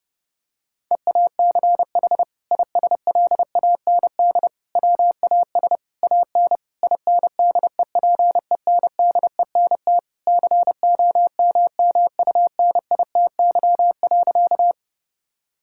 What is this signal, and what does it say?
Morse code, heard as a keyed tone, about 32 wpm, 700 Hz. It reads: EACH ISLAND WAS AN INDEPENDENT COMMUNITY.
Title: Sentences from Top 1000 Words - Review Part 6 - 30wpm